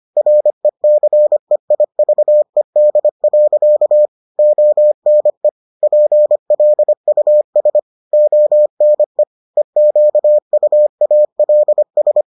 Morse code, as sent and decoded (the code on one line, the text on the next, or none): .-. -.-. .. ...- . -.. .-.-.- / --- -. / .--. .-.. ..- ... / --- -. / . --.- ..- .- .-.. ...
RECEIVED. ONE PLUS ONE EQUALS